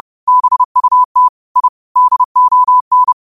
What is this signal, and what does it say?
Morse code, heard as a keyed tone, about 30 wpm, 1000 Hz. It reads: BUT I DON
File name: qotdfiles / 30wpm